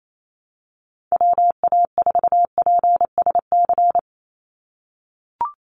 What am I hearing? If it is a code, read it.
Morse code, heard as a keyed tone, about 28 wpm, 700 Hz. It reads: WA4PSC
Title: Call Signs - 28wpm